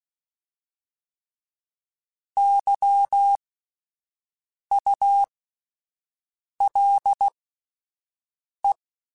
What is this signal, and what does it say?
Morse code, heard as a keyed tone, about 16 wpm, spaced as for 5 wpm, 800 Hz. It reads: YULE
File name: Alphabetical / Alphabetical mp3s / Ys @5